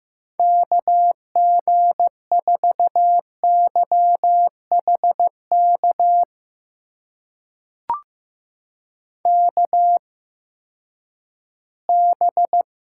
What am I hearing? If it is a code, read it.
Morse code, heard as a keyed tone, about 15 wpm, spaced as for 10 wpm, 700 Hz. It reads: KG4YHK KB
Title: USA Call Signs - 6 Characters - 8x Character Spacing - 15wpm